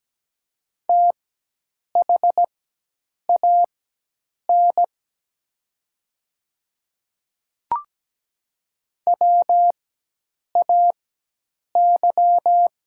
Morse code, heard as a keyed tone, about 17 wpm, spaced as for 8 wpm, 700 Hz. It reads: THAN WAY
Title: 4x Character Spacing - Top 100 Words - 17wpm